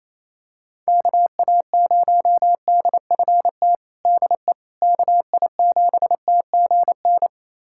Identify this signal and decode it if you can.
Morse code, heard as a keyed tone, about 28 wpm, 700 Hz. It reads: KA0DFT DE KI7TGN
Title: USA Callsign DE Callsign - 6 Characters - 28wpm